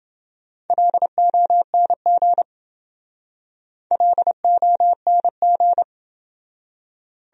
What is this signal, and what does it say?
Morse code, heard as a keyed tone, about 27 wpm, 700 Hz. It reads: LONG LONG